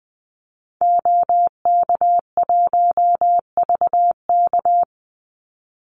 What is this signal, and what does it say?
Morse code, heard as a keyed tone, about 20 wpm, 700 Hz. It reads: OK1VK
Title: Call Signs - 20wpm